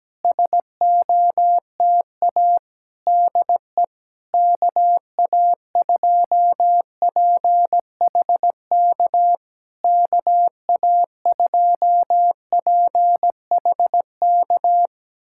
Morse code, heard as a keyed tone, about 17 wpm, 700 Hz. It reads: SOTA DE KA2PHK KA2PHK
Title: SOTA QSO Practice 17wpm